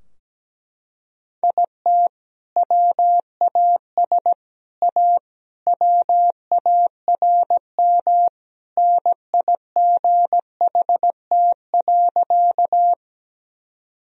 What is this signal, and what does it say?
Morse code, heard as a keyed tone, about 17 wpm, 700 Hz. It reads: IT WAS A WARM NIGHT.
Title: Sentences from Top 1000 Words - Review Part 2 - 17wpm